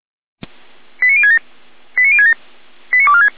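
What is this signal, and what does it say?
Five-tone selective call (ZVEI), heard as quick runs of five tones, tones of about 80 ms. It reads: 89067 89067 89246
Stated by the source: ZVEI five-tone selective call received by SDR.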